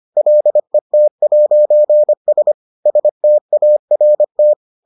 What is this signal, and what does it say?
Morse code, heard as a keyed tone, about 25 wpm, 600 Hz. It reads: LET'S START